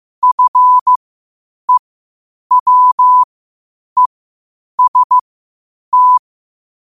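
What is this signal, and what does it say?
Morse code, heard as a keyed tone, about 15 wpm, spaced as for 8 wpm, 1000 Hz. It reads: FEWEST
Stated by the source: Morse code quotes at 8 words per minute from QOTD at 8 WPM.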